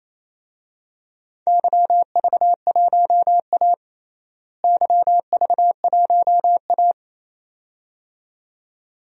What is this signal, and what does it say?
Morse code, heard as a keyed tone, about 28 wpm, 700 Hz. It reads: YV1A YV1A
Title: DXCC-100 Call Signs - 4 Characters - Repeated 2x - 28wpm